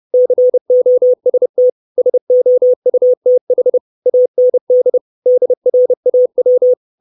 Morse code, heard as a keyed tone, about 30 wpm, 500 Hz. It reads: COST SOUTH AND DRAW
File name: 632 Words 30WPM